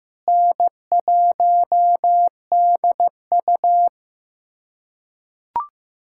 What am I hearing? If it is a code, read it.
Morse code, heard as a keyed tone, about 15 wpm, 700 Hz. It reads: N1DU
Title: USA Callsign DE Callsign - 4 Characters - 15wpm